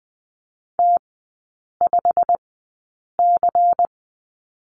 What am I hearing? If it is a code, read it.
Morse code, heard as a keyed tone, about 20 wpm, spaced as for 8 wpm, 700 Hz. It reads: T5C
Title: 100 Lesson 17e - Letter u - Three Characters 20wpm